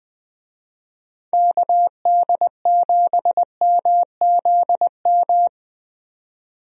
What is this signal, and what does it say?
Morse code, heard as a keyed tone, about 21 wpm, 700 Hz. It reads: KD7MZM
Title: USA Call Signs - 6 Characters - 20wpm